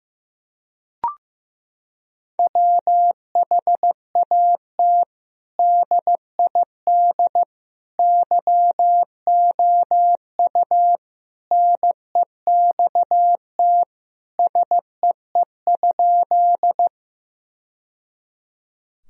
Morse code, heard as a keyed tone, about 16 wpm, 700 Hz. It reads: WHAT DID YOU NEXT SEE?
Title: Sentences from Top 1000 Words - Review Part 1 - 15wpm